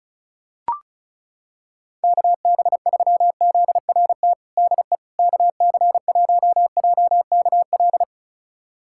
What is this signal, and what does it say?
Morse code, heard as a keyed tone, about 36 wpm, 700 Hz. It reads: KB3ZRT DE KC1JKL
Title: USA Callsign DE Callsign 35wpm